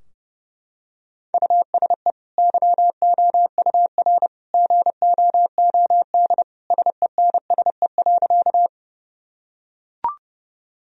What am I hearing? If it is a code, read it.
Morse code, heard as a keyed tone, about 30 wpm, 700 Hz. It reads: USE YOUR GOOD SENSE.
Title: Sentences from Top 1000 Words - Review Part 4 - 30wpm